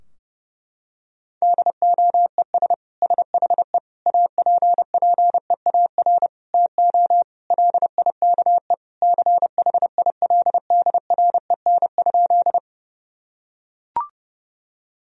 Morse code, heard as a keyed tone, about 30 wpm, 700 Hz. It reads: DOES SHE APPEAR TO LIKE CHILDREN?
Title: Sentences from Top 1000 Words - Review Part 2 - 30wpm